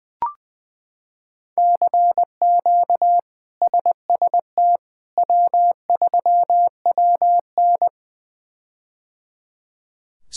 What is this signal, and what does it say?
Morse code, heard as a keyed tone, about 20 wpm, 700 Hz. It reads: CQ SST W3WN
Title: K1USN SST QSO Practice - 20wpm